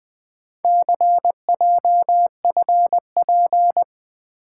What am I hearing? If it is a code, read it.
Morse code, heard as a keyed tone, about 20 wpm, 700 Hz. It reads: CJFP
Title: Four Letter-Number Combinations - 20wpm